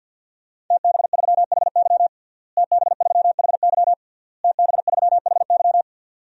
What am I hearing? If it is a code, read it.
Morse code, heard as a keyed tone, about 53 wpm, 700 Hz. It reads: TB3HY TB3HY TB3HY
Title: DXCC-100 Call Signs - 5 Characters - Repeated 3x - 50wpm